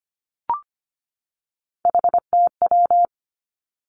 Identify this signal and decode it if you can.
Morse code, heard as a keyed tone, about 25 wpm, 700 Hz. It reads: HTW
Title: Three Challenging Letter-Number Combinations 25wpm